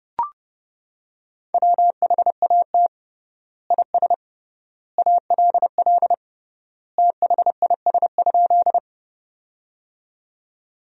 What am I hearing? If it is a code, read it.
Morse code, heard as a keyed tone, about 30 wpm, 700 Hz. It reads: WHAT IS ALL THIS?
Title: 3x Word-Spacing - Sentences from Top 100 Words - 30wpm